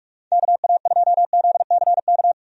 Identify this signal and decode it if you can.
Morse code, heard as a keyed tone, about 46 wpm, 700 Hz. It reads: KA2ZCK
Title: USA Call Signs - 6 Characters - 45wpm